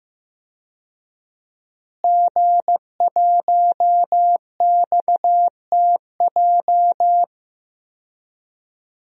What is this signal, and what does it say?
Morse code, heard as a keyed tone, about 15 wpm, 700 Hz. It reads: G1XTJ
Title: DXCC-100 Call Signs - 5 Characters - Repeated 3x - 15wpm